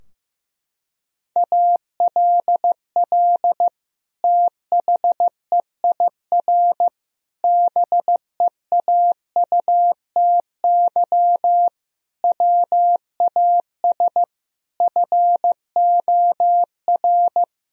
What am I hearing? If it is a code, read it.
Morse code, heard as a keyed tone, about 16 wpm, 700 Hz. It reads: ALL THEIR BEAUTY WAS FOR
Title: Sentences from Top 1000 Words - Review Part 2 - 15wpm